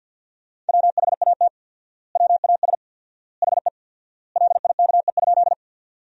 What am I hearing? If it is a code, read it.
Morse code, heard as a keyed tone, about 50 wpm, 700 Hz. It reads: WHAT WAS HE LIKE?
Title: 4x Word-Spacing - Sentences from Top 100 Words - 50wpm